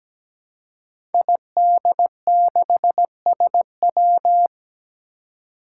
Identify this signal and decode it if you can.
Morse code, heard as a keyed tone, about 17 wpm, 700 Hz. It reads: ID6SW
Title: Five Challenging Letter-Number Combinations 17wpm